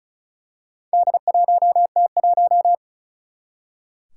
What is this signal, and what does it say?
Morse code, heard as a keyed tone, about 34 wpm, 700 Hz. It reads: D1T1
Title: Four Challenging Letter-Number Combinations 35wpm